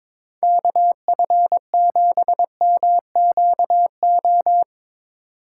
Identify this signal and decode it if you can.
Morse code, heard as a keyed tone, about 22 wpm, 700 Hz. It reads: KF7MQO
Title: USA Call Signs - 6 Characters - 6x Character Spacing - 22wpm